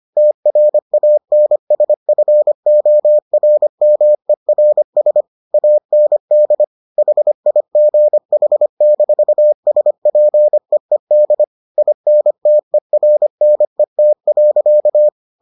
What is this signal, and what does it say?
Morse code, heard as a keyed tone, about 25 wpm, 600 Hz. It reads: TRANSFORMERS AND HIGH-SPEED INTERNET.